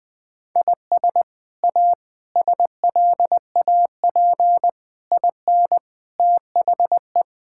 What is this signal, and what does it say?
Morse code, heard as a keyed tone, about 20 wpm, 700 Hz. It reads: IS A SLAP IN THE
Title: Sets of 8 Words - 20wpm